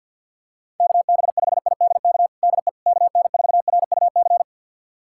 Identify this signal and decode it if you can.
Morse code, heard as a keyed tone, about 45 wpm, 700 Hz. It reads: KB5IDK DE KN4RUC
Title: USA Callsign DE Callsign - 6 Characters - 50wpm